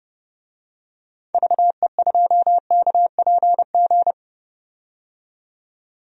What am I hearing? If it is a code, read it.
Morse code, heard as a keyed tone, about 32 wpm, 700 Hz. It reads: VE2KPG
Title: DXCC-100 Call Signs - 6 Characters - 30wpm